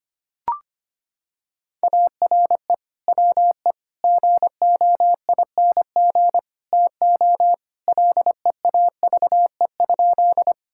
Morse code, heard as a keyed tone, about 25 wpm, 700 Hz. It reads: ARE WE GOING TO LEAVE?
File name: practice / 25wpm / Sentences with 4-Word Phrases - 25wpm